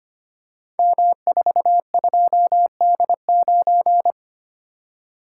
Morse code, heard as a keyed tone, about 25 wpm, 700 Hz. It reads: M42D9
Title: Code Groups - 5 Letter:Numbers - 25wpm